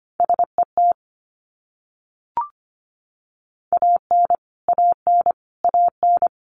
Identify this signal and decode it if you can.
Morse code, heard as a keyed tone, about 25 wpm, 700 Hz. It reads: SET AN AN AN
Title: Top 200 Words Repeated 25wpm